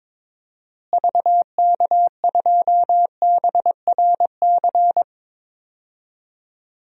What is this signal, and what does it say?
Morse code, heard as a keyed tone, about 22 wpm, 700 Hz. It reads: VK2BRC
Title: DXCC-100 Call Signs - 6 Characters - Repeated 3x - 22wpm